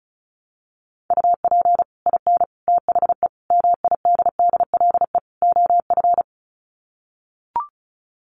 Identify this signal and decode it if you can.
Morse code, heard as a keyed tone, about 34 wpm, 700 Hz. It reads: UP IN THE MIDDLE OF
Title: Sets of 5 Words Encore 35wpm